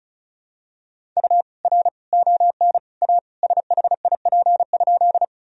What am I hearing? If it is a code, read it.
Morse code, heard as a keyed tone, about 36 wpm, 700 Hz. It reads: U R ON A SHIP?
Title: Sentences from Top 400 Words and Abbreviations 35wpm